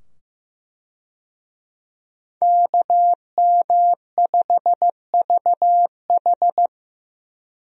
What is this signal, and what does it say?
Morse code, heard as a keyed tone, about 15 wpm, 700 Hz. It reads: KM5VH
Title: USA Call Signs - 5 Characters - 15wpm